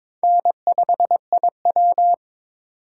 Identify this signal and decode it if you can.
Morse code, heard as a keyed tone, about 22 wpm, 700 Hz. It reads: N5IW
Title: USA Call Signs - 4 Characters - 2x Character Spacing - 22wpm